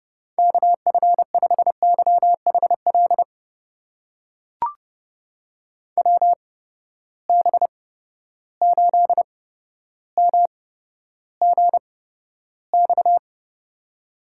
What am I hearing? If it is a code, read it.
Morse code, heard as a keyed tone, about 31 wpm, spaced as for 13 wpm, 700 Hz. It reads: KF5YHL WB8MGX
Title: USA Call Signs - 6 Characters - 8x Character Spacing - 30wpm